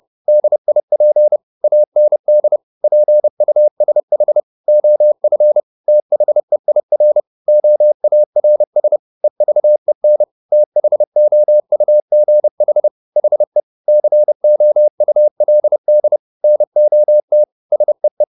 Morse code, heard as a keyed tone, about 30 wpm, 600 Hz. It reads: DIP AND PUSH OF THEIR OARS EVEN THOUGH HE COULD NOT SEE
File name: the old man and the sea / the old man and the sea 1 30wpm